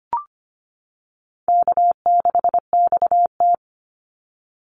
Morse code, heard as a keyed tone, about 25 wpm, 700 Hz. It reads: K6XT
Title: Easy Call Signs - 25wpm